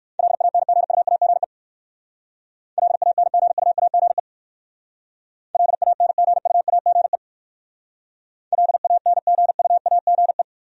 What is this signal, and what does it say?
Morse code, heard as a keyed tone, about 58 wpm, 700 Hz. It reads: LANGUAGE LANGUAGE LANGUAGE LANGUAGE